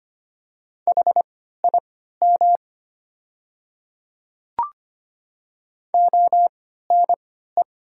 Morse code, heard as a keyed tone, about 25 wpm, spaced as for 14 wpm, 700 Hz. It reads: HIM ONE
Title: 3x Character Spacing - Top 300 Words - 25wpm